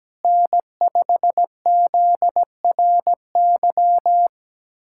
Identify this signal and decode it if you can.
Morse code, heard as a keyed tone, about 17 wpm, 700 Hz. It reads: N5ZRY
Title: USA Callsign DE Callsign - 5 Characters - 17wpm